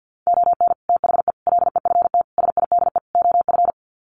Morse code, heard as a keyed tone, about 49 wpm, 700 Hz. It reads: ON THE LEFT SIDE OF
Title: Sets of 5 Words Encore2 50wpm